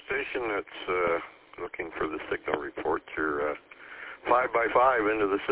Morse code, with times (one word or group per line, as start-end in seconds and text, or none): none